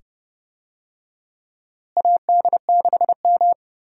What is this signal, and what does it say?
Morse code, heard as a keyed tone, about 30 wpm, 700 Hz. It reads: AD6M
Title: USA Call Signs - 4 Characters - Repeated 2x - 30wpm